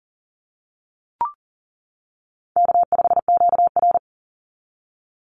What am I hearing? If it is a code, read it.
Morse code, heard as a keyed tone, about 40 wpm, 700 Hz. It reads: K5QR
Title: NA SOTA Chaser Callsigns 40wpm